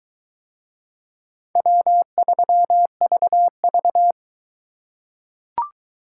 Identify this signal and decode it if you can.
Morse code, heard as a keyed tone, about 22 wpm, 700 Hz. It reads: W3VV